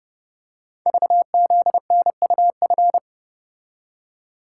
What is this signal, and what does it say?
Morse code, heard as a keyed tone, about 29 wpm, 700 Hz. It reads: VZNUF
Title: Code Groups - 5 Letter:Numbers - 30wpm